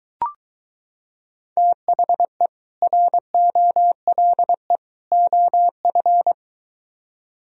Morse code, heard as a keyed tone, about 23 wpm, 700 Hz. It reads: THE ROLE OF